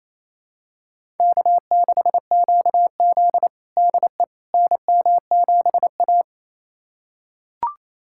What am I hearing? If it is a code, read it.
Morse code, heard as a keyed tone, about 28 wpm, 700 Hz. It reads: K6QZ DE NM7A
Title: USA Callsign DE Callsign - 4 Characters - 28wpm